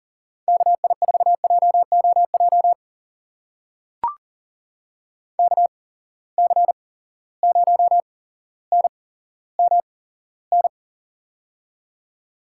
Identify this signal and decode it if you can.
Morse code, heard as a keyed tone, about 38 wpm, spaced as for 17 wpm, 700 Hz. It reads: KI4JOJ KC0NMN
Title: USA Call Signs - 6 Characters - 8x Character Spacing - 40wpm